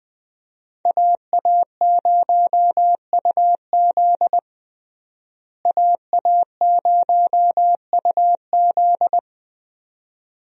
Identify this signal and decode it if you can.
Morse code, heard as a keyed tone, about 20 wpm, 700 Hz. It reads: AA0UZ AA0UZ
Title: USA Call Signs - 5 Characters - Repeated 2x - 20wpm